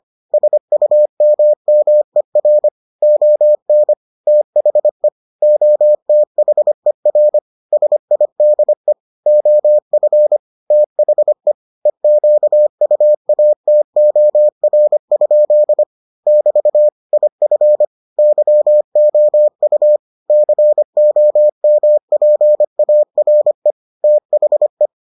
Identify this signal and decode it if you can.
Morse code, heard as a keyed tone, about 25 wpm, 600 Hz. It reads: SUMMER ON THE OTHER SIDE OF THE EQUATOR? <BT> IF YOU COMPARE THE